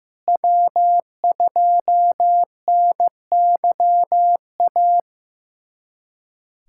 Morse code, heard as a keyed tone, about 15 wpm, 700 Hz. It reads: W2NYA
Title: USA Callsign DE Callsign - 5 Characters - 15wpm